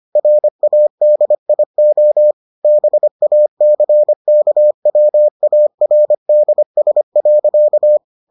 Morse code, heard as a keyed tone, about 25 wpm, 600 Hz. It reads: RADIO BACKWARDS.